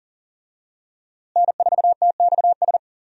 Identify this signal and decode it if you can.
Morse code, heard as a keyed tone, about 40 wpm, 700 Hz. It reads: N4TXS
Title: USA Call Signs - 5 Characters - Repeated 3x - 40wpm